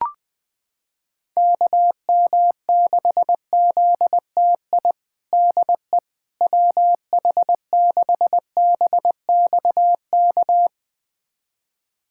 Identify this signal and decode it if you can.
Morse code, heard as a keyed tone, about 20 wpm, 700 Hz. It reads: KM6ZTI DE WH6BXK
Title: USA Callsign DE Callsign - 6 Characters - 20wpm